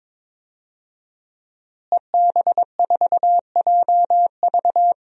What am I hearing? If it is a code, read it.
Morse code, heard as a keyed tone, about 22 wpm, 700 Hz. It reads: EB4JV
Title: DXCC-100 Call Signs - 5 Characters - Repeated 3x - 22wpm